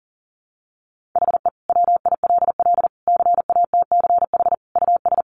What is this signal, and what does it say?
Morse code, heard as a keyed tone, about 40 wpm, 700 Hz. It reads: HE WILL CATCH US
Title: Sentences from Top 1500 Words 40wpm